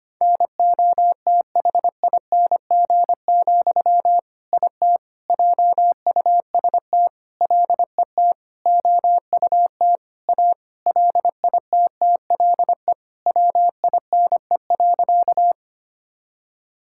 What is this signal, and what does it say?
Morse code, heard as a keyed tone, about 25 wpm, 700 Hz. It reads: NOTHING, IT JUST LET OUT A LITTLE WINE.
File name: practice / 25wpm / All-Age Appropriate Jokes - 25wpm